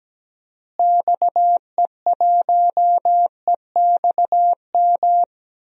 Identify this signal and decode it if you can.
Morse code, heard as a keyed tone, about 17 wpm, 700 Hz. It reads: XE1EXM
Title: DXCC-100 Call Signs - 6 Characters - Repeated 3x - 17wpm